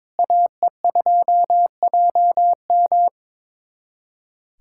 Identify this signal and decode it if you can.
Morse code, heard as a keyed tone, about 22 wpm, 700 Hz. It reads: AE2JM
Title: USA Callsign DE Callsign 22wpm